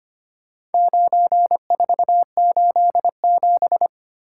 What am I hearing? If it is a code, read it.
Morse code, heard as a keyed tone, about 25 wpm, 700 Hz. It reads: 9487
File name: practice / 25wpm / Four Letter-Number Combinations - 25wpm